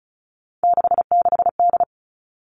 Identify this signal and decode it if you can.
Morse code, heard as a keyed tone, about 34 wpm, 700 Hz. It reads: 66D
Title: All Cut Numbers - 3-Digit 35wpm